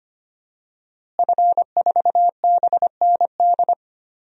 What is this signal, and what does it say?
Morse code, heard as a keyed tone, about 25 wpm, 700 Hz. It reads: F4BND
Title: DXCC-100 Call Signs - 5 Characters - Repeated 3x - 25wpm